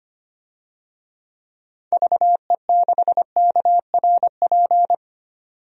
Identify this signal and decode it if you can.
Morse code, heard as a keyed tone, about 25 wpm, 700 Hz. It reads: VE6KRP
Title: DXCC-100 Call Signs - 6 Characters - 25wpm